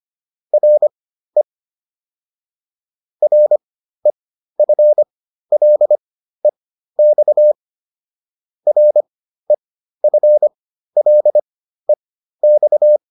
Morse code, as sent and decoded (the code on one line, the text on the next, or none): .-. / .-. ..-. .-.. -..- / .-. ..-. .-.. -..-
RE REFLEX REFLEX